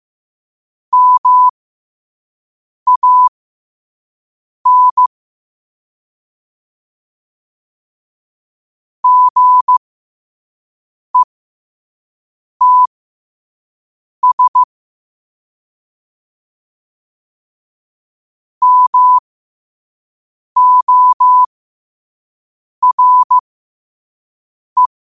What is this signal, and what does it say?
Morse code, heard as a keyed tone, about 15 wpm, spaced as for 5 wpm, 1000 Hz. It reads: MAN GETS MORE
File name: qotdfiles / 5wpm